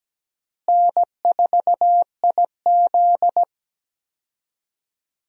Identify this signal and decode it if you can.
Morse code, heard as a keyed tone, about 17 wpm, 700 Hz. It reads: N4IZ
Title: USA Call Signs - 4 Characters - Repeated 2x - 17wpm